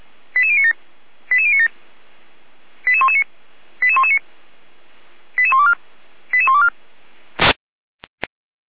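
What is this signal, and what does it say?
Five-tone selective call (ZVEI), heard as quick runs of five tones, tones of about 70 ms. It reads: 88097 88097 88119 88119 88124 88124
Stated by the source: ZVEI five-tone call 88057 received on an SDR.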